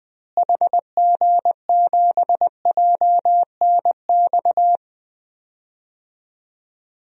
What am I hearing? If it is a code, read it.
Morse code, heard as a keyed tone, about 21 wpm, 700 Hz. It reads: HG7JNX
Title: DXCC-100 Call Signs - 6 Characters - Repeated 3x - 20wpm